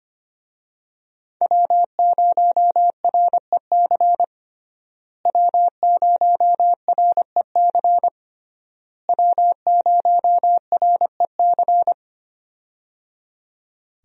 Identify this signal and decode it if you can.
Morse code, heard as a keyed tone, about 25 wpm, 700 Hz. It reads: W0REC W0REC W0REC
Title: USA Call Signs - 5 Characters - Repeated 3x - 25wpm